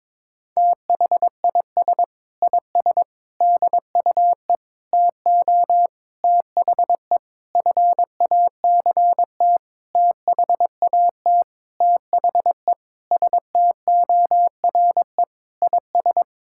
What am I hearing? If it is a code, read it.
Morse code, heard as a keyed tone, about 22 wpm, 700 Hz. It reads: THIS IS DUE TO THE FACT THAT THE STORE IS
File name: Sentences with 7-Word Phrases - 22wpm